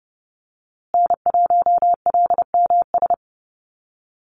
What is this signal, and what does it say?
Morse code, heard as a keyed tone, about 30 wpm, 700 Hz. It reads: N1LMS
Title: USA Call Signs Repeated 2x 30wpm